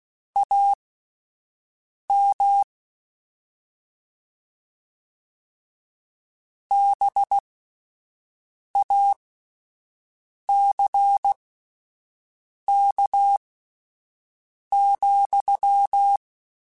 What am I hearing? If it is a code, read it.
Morse code, heard as a keyed tone, about 16 wpm, spaced as for 5 wpm, 800 Hz. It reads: AM BACK,
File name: QSOs Spaced